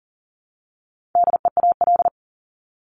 Morse code, heard as a keyed tone, about 40 wpm, 700 Hz. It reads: DEAL
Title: ING Suffix - Top 100 Words with Base Words 40wpm